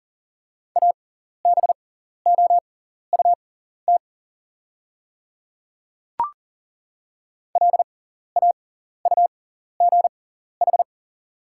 Morse code, heard as a keyed tone, about 40 wpm, spaced as for 14 wpm, 690 Hz. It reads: ABOUT LAUGH
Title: 6x Character Spacing - Top 500 Words - 40wpm